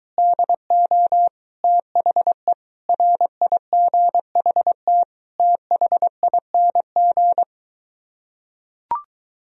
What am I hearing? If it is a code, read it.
Morse code, heard as a keyed tone, about 23 wpm, 700 Hz. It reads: DO THE RIGHT THING